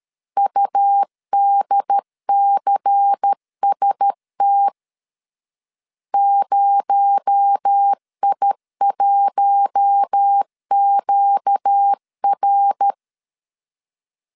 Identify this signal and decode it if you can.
Morse code, heard as a keyed tone, about 13 wpm, 800 Hz. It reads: UDCST 0I1QR
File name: CW / DET / 1 6 1500C S A 12wpm 3440